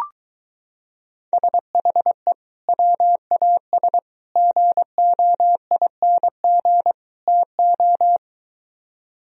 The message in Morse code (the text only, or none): SHE WAS GOING TO